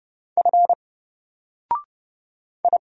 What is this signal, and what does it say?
Morse code, heard as a keyed tone, about 31 wpm, 700 Hz. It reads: F I
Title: Single Character (Letter:Number:Punctuation:Prosign) 30wpm